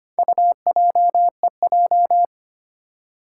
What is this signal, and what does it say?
Morse code, heard as a keyed tone, about 25 wpm, 700 Hz. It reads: UJEJ
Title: Four Challenging Letter-Number Combinations 25wpm